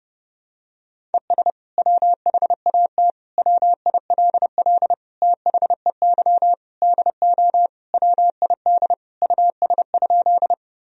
Morse code, heard as a keyed tone, about 32 wpm, 700 Hz. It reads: ES WHAT WILL THEY DO WID US?
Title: Sentences from Top 200 Words and Abbreviations 30wpm